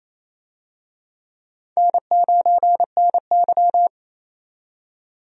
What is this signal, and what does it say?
Morse code, heard as a keyed tone, about 28 wpm, 700 Hz. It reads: N9NY
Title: USA Call Signs - 4 Characters - 28wpm